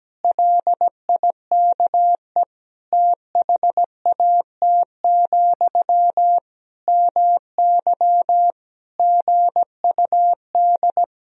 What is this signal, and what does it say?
Morse code, heard as a keyed tone, about 17 wpm, 700 Hz. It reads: LIKE THAT, MY GUD
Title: Sentences from Top 400 Words and Abbreviations 17wpm